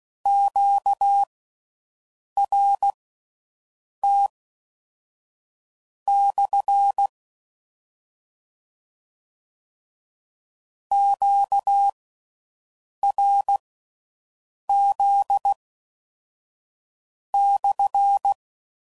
Morse code, heard as a keyed tone, about 16 wpm, spaced as for 5 wpm, 800 Hz. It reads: QRT/ QRZ/